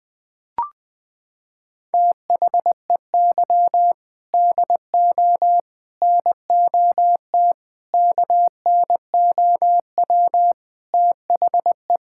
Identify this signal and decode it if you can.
Morse code, heard as a keyed tone, about 21 wpm, 700 Hz. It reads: THEY DO NOT KNOW THE
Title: Sentences from Top 1000 Words - Review Part 6 - 20wpm